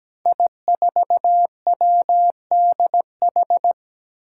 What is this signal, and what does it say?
Morse code, heard as a keyed tone, about 17 wpm, 700 Hz. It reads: I4WDH
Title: Five Challenging Letter-Number Combinations 17wpm